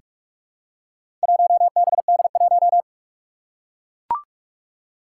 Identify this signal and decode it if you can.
Morse code, heard as a keyed tone, about 44 wpm, 700 Hz. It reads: 1BD1
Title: Four Challenging Letter-Number Combinations 45wpm